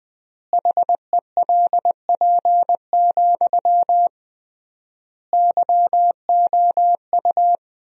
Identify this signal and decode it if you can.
Morse code, heard as a keyed tone, about 20 wpm, 700 Hz. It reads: HELP, YOU
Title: 3x Word-Spacing - Sentences from Top 300 Words - 20wpm